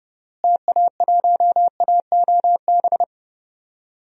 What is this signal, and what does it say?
Morse code, heard as a keyed tone, about 30 wpm, 700 Hz. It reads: TA1AOB
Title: DXCC-100 Call Signs - 6 Characters - Repeated 2x - 30wpm